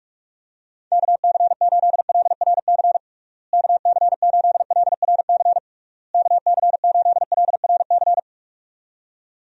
Morse code, heard as a keyed tone, about 41 wpm, 700 Hz. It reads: KC8LRC KC8LRC KC8LRC
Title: USA Call Signs - 6 Characters - Repeated 3x - 45wpm